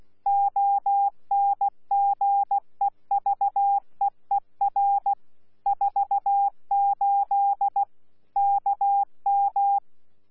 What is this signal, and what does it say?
Morse code, heard as a keyed tone, about 16 wpm, 800 Hz. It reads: ONGEVEER 48 KM
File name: Z PI4VRZ / PI4VRZ 14-jun-2025